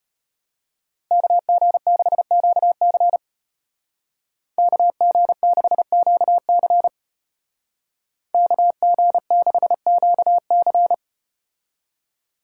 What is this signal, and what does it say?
Morse code, heard as a keyed tone, about 34 wpm, 700 Hz. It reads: KG6QC KG6QC KG6QC